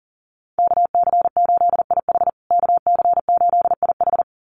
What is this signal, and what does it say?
Morse code, heard as a keyed tone, about 40 wpm, 700 Hz. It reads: KC8IH KC8IH
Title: USA Call Signs Repeated 2x 40wpm